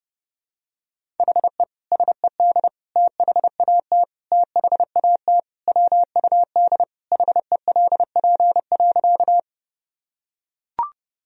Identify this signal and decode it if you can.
Morse code, heard as a keyed tone, about 32 wpm, 700 Hz. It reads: HE SED THAT THAT WUD HELP.
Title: Sentences from Top 200 Words and Abbreviations 30wpm